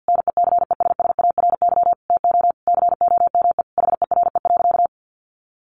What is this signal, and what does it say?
Morse code, heard as a keyed tone, about 50 wpm, 700 Hz. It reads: NECESSARY TO COME HERE.
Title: Sentences from Top 1300 Words 50wpm